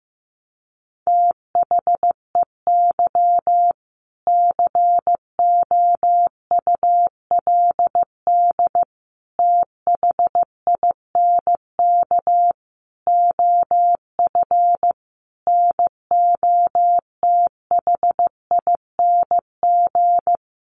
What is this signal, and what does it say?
Morse code, heard as a keyed tone, about 15 wpm, 700 Hz. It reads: THEY COULD THINK OF NOTHING
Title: Sentences from Top 800 Words 15wpm